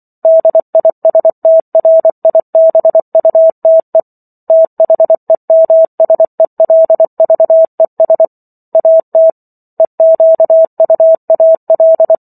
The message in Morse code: DISTRIBUTE THEMSELVES AT EQUAL

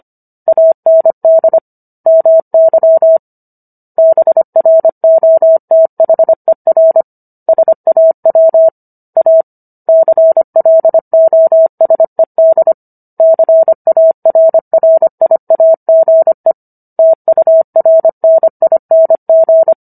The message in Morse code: AND MY BROTHER SAW A CLOSED CARRIAGE TURNING

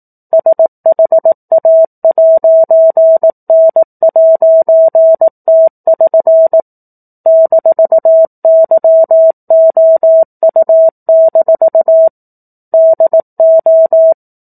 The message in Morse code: SHA'N'T<SN> -YOU- DO

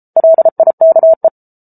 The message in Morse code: LIKE

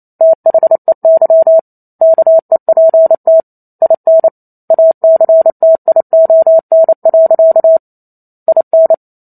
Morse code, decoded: THEY KEPT IN ACTION. IN

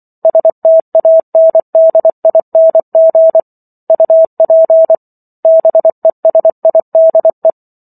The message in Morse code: STANDING UP BESIDE

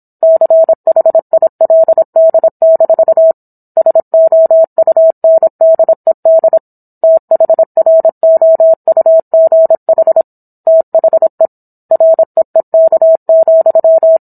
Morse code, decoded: CHILD- SOUNDED THROUGH THE REEK,

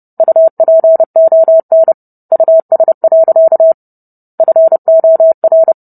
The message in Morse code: UPON US. FOR

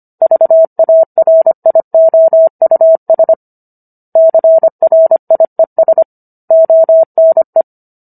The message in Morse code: VARIOUS CRIES ONE

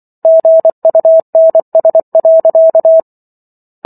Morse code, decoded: GUNS.

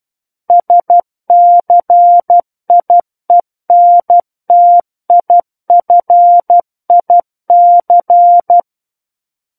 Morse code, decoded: SCIENTIFIC

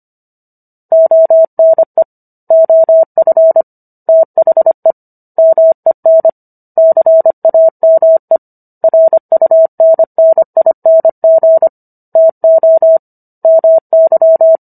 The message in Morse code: ONE OF THE MEN CAME RUNNING TO MY

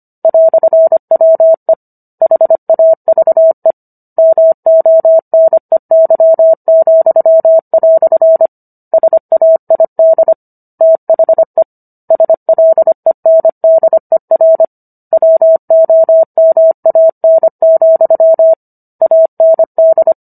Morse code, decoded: "WE HAVE MONEY," SAID THE SLENDER WOMAN, AND

